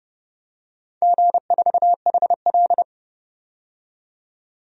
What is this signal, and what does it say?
Morse code, heard as a keyed tone, about 30 wpm, 700 Hz. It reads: G4HL
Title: DXCC-100 Call Signs - 4 Characters - 30wpm